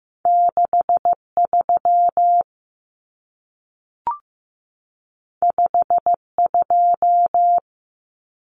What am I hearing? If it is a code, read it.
Morse code, heard as a keyed tone, about 15 wpm, 700 Hz. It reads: 63 52
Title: Numbers - 2-Digit 15wpm